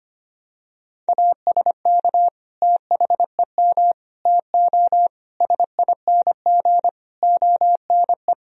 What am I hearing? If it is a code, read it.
Morse code, heard as a keyed tone, about 25 wpm, 700 Hz. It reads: ASK THEM TO SING ONE
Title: Sentences from Top 1000 Words - Review Part 2 - 25wpm